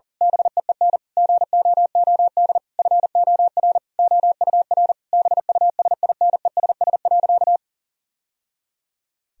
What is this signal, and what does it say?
Morse code, heard as a keyed tone, about 40 wpm, 700 Hz. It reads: BEEN GOOD FOR OUR BUSINESS.
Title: Sentences from Top 1000 Words - Review Part 4 - 40wpm